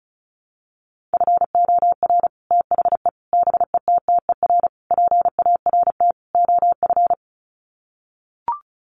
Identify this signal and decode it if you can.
Morse code, heard as a keyed tone, about 34 wpm, 700 Hz. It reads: FOR THE BETTER PART OF
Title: Sets of 5 Words Encore2 35wpm